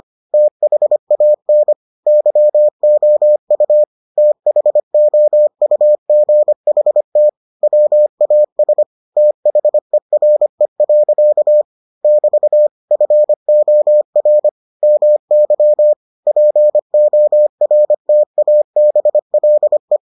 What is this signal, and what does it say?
Morse code, heard as a keyed tone, about 25 wpm, 600 Hz. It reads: THAN YOU THOUGHT WAS THERE. <BT> FOR MY PORTABLE